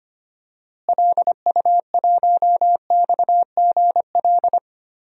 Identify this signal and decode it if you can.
Morse code, heard as a keyed tone, about 25 wpm, 700 Hz. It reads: LU1XGL
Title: DXCC-100 Call Signs - 6 Characters - Repeated 3x - 25wpm